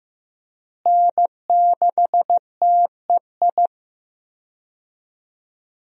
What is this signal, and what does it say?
Morse code, heard as a keyed tone, about 15 wpm, 700 Hz. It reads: N6TEI
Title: USA Call Signs - 5 Characters - Repeated 2x - 15wpm